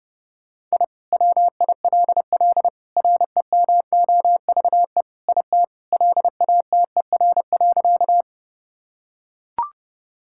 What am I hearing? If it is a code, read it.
Morse code, heard as a keyed tone, about 30 wpm, 700 Hz. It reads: I WILL REMOVE IT LATER.
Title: Sentences from Top 1000 Words - Review Part 5 - 30wpm